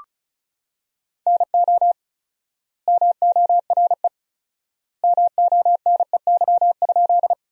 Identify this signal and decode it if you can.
Morse code, heard as a keyed tone, about 36 wpm, 700 Hz. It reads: NO MORE MONEY?
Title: 4x Word-Spacing - Sentences from Top 400 Words - 35wpm